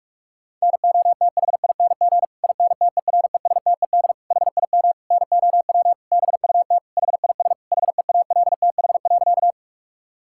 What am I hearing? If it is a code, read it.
Morse code, heard as a keyed tone, about 41 wpm, 700 Hz. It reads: NOTHING INTERESTED HIM NOW BUT HIS HEALTH.
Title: Sentences from Top 1000 Words - Review Part 6 - 45wpm